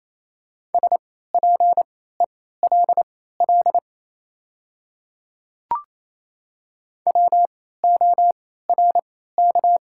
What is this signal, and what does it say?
Morse code, heard as a keyed tone, about 28 wpm, spaced as for 16 wpm, 700 Hz. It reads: SPELL WORK
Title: 3x Character Spacing - Top 200 Words - 28wpm